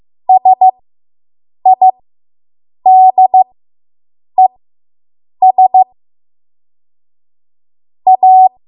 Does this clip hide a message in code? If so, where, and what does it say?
Yes: Morse code from 0.3 to 8.5 s, heard as a keyed tone, about 15 wpm, spaced as for 7 wpm, 750 Hz. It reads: SIDES A